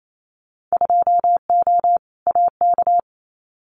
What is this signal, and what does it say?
Morse code, heard as a keyed tone, about 28 wpm, 700 Hz. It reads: 2O AK
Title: Winter Field Day Exchanges Encore 28wpm